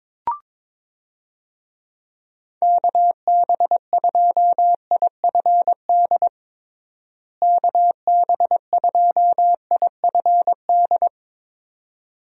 Morse code, heard as a keyed tone, about 22 wpm, 700 Hz. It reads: KB2IFD KB2IFD
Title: USA Call Signs - 6 Characters - Repeated 2x - 22wpm